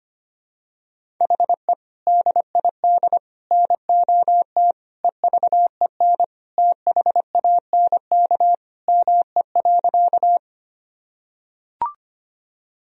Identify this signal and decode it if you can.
Morse code, heard as a keyed tone, about 25 wpm, 700 Hz. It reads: HE DID NOT EVEN THANK ME.
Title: Sentences from Top 1000 Words - Review Part 1 - 25wpm